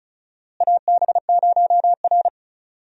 Morse code, heard as a keyed tone, about 34 wpm, 700 Hz. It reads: AB0R
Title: USA Call Signs - 4 Characters - Repeated 2x - 35wpm